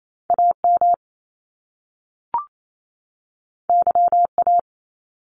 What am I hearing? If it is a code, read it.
Morse code, heard as a keyed tone, about 28 wpm, 700 Hz. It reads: AM YA
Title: Top Two Letter Combinations in Words 28wpm